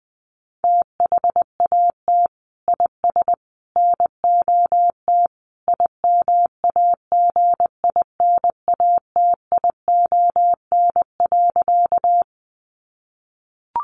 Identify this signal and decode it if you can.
Morse code, heard as a keyed tone, about 20 wpm, 700 Hz. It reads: THAT IS NOT IMAGINATION.
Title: Sentences from Top 1300 Words 20wpm